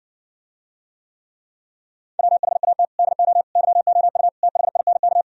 Morse code, heard as a keyed tone, about 69 wpm, 700 Hz. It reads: WHAT DO YOU THINK